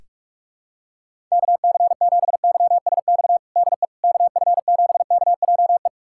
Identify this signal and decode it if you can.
Morse code, heard as a keyed tone, about 46 wpm, 700 Hz. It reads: KC7YSX DE KF7KJE
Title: USA Callsign DE Callsign - 6 Characters - 45wpm